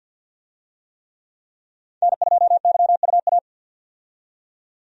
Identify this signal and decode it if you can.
Morse code, heard as a keyed tone, about 51 wpm, 700 Hz. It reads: N2YUA
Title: USA Call Signs - 5 Characters - 50wpm